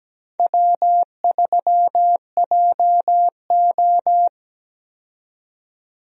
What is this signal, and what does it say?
Morse code, heard as a keyed tone, about 17 wpm, 700 Hz. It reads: W3JO
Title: USA Call Signs - 4 Characters - 17wpm